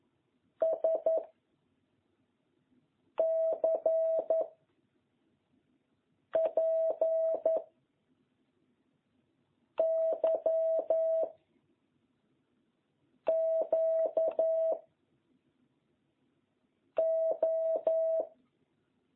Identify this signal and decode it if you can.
Morse code, heard as a keyed tone, about 11 wpm, 650 Hz. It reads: S C P Y Q O